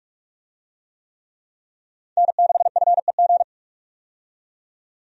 Morse code, heard as a keyed tone, about 45 wpm, 700 Hz. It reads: N6FEG
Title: USA Call Signs - 5 Characters - 45wpm